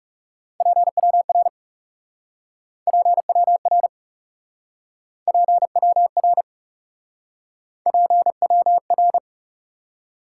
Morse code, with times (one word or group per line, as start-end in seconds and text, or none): 0.6-1.5 PWR
2.9-3.9 PWR
5.3-6.4 PWR
7.9-9.2 PWR